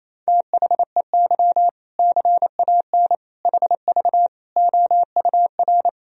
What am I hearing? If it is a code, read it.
Morse code, heard as a keyed tone, about 27 wpm, 700 Hz. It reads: THEY CAN HV OUR
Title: Sentences from Top 400 Words and Abbreviations 28wpm